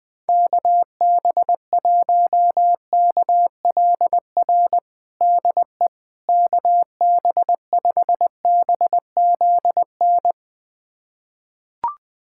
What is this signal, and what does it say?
Morse code, heard as a keyed tone, about 20 wpm, 700 Hz. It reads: KB1KLR DE KB5BZN
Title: USA Callsign DE Callsign - 6 Characters - 20wpm